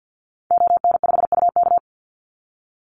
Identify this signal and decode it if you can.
Morse code, heard as a keyed tone, about 50 wpm, 700 Hz. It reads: ON5UK
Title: Call Signs - 50wpm